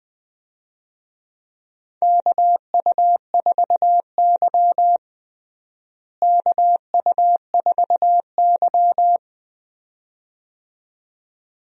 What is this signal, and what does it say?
Morse code, heard as a keyed tone, about 20 wpm, 700 Hz. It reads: KU4Y KU4Y
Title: USA Call Signs - 4 Characters - Repeated 2x - 20wpm